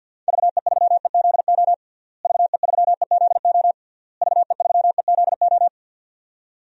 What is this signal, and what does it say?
Morse code, heard as a keyed tone, about 52 wpm, 700 Hz. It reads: VE3EZO VE3EZO VE3EZO
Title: DXCC-100 Call Signs - 6 Characters - Repeated 3x - 50wpm